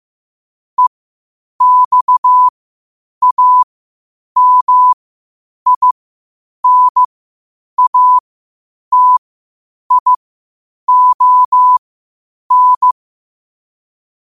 Morse code, heard as a keyed tone, about 15 wpm, spaced as for 8 wpm, 1000 Hz. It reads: EXAMINATION
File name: qotdfiles / 8wpm